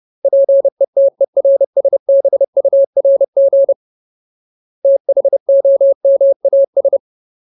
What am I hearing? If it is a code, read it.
Morse code, heard as a keyed tone, about 30 wpm, 550 Hz. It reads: PETERSBURG THOMAS
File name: Ultimate Word List / 30WPM Ultimate